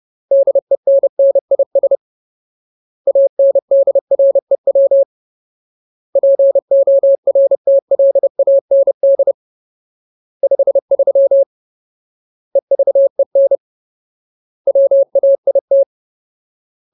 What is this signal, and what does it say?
Morse code, heard as a keyed tone, about 30 wpm, 550 Hz. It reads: DENNIS ANDREW PORTLAND 53 EVEN WAIT
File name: Ultimate Word List / 30WPM Ultimate